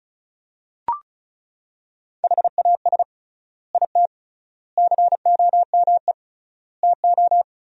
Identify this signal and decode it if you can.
Morse code, heard as a keyed tone, about 36 wpm, 700 Hz. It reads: HAS IT COME TO
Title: 3x Word-Spacing - Sentences from Top 100 Words - 35wpm